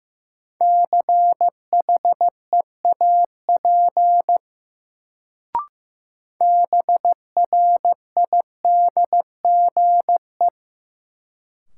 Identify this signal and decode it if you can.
Morse code, heard as a keyed tone, about 15 wpm, 700 Hz. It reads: CHEAP BRIDGE
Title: Sentences from Top 1200 Words - New Word Drill - 15wpm